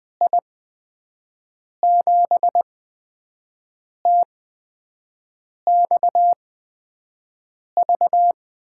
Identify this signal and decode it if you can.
Morse code, heard as a keyed tone, about 20 wpm, spaced as for 5 wpm, 700 Hz. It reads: I7TXV
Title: USA Call Signs - 6 Characters - 8x Character Spacing - 20wpm